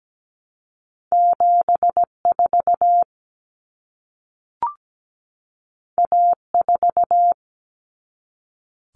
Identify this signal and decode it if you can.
Morse code, heard as a keyed tone, about 17 wpm, 700 Hz. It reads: 74 A4
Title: Common Cut Numbers - 2-Digit 17wpm